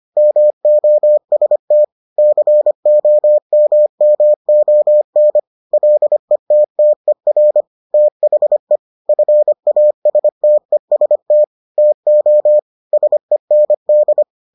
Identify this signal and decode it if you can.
Morse code, heard as a keyed tone, about 25 wpm, 600 Hz. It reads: MOST COMMON LETTER THE FASTEST TO SEND